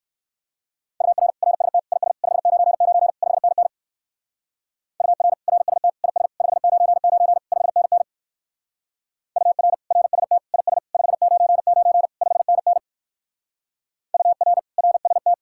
Morse code, heard as a keyed tone, about 58 wpm, 700 Hz. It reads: UR RST IS 599 5NN UR RST IS 599 5NN UR RST IS 599 5NN UR RST